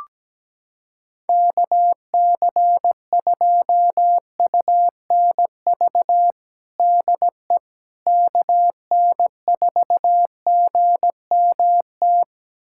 Morse code, heard as a keyed tone, about 17 wpm, 700 Hz. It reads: KC2UNV DE KN4GMT
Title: USA Callsign DE Callsign - 6 Characters - 17wpm